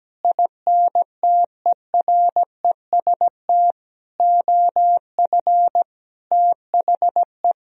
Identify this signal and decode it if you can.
Morse code, heard as a keyed tone, about 17 wpm, 700 Hz. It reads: INTEREST OF THE
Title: Sentences from Top 1000 Words - Review Part 4 - 17wpm